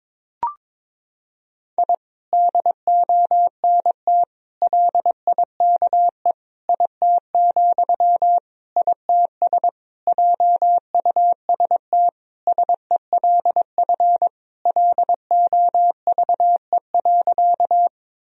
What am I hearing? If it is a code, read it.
Morse code, heard as a keyed tone, about 22 wpm, 700 Hz. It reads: I DONT LIKE IT, ITS JUST SELF LOVE.
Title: Sentences from Top 1000 Words - Review Part 1 - 22wpm